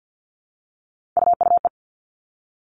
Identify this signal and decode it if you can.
Morse code, heard as a keyed tone, about 50 wpm, 700 Hz. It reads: UUE
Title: All Cut Numbers - 3-Digit 50wpm